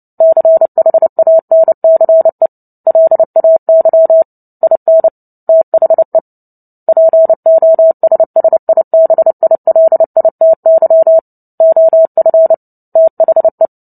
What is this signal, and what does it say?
Morse code, heard as a keyed tone, about 30 wpm, 650 Hz. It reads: CHANCE LAY IN THE POSSIBILITY OF THE
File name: Book Two / 03-Chapter 3 - 29 WPM